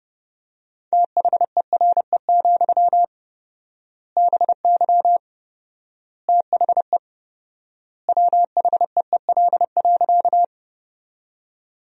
Morse code, heard as a keyed tone, about 30 wpm, 700 Hz. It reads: THERE, BY THE WHEEL.
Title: 4x Word-Spacing - Sentences from Top 500 Words - 30wpm